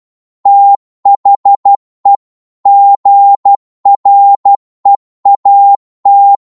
Morse code, heard as a keyed tone, about 12 wpm, 800 Hz. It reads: THE GREAT